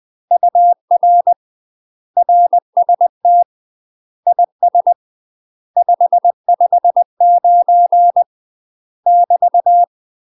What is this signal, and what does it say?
Morse code, heard as a keyed tone, about 20 wpm, 700 Hz. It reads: UR RST IS 559 <BT>
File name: UK MASTER QSOs TRAINING / 20WPM / UK MASTER QSOs 0044